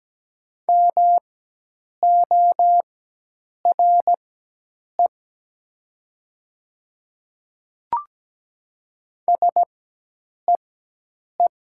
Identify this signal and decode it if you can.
Morse code, heard as a keyed tone, about 17 wpm, spaced as for 8 wpm, 700 Hz. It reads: MORE SEE
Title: 4x Character Spacing - Top 100 Words - 17wpm